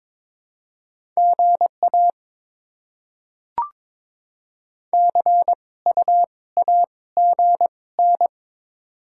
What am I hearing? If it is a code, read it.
Morse code, heard as a keyed tone, about 22 wpm, spaced as for 17 wpm, 690 Hz. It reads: GA CU AGN
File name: Top 20 QSO Elements - 2x Character Spacing - 22wpm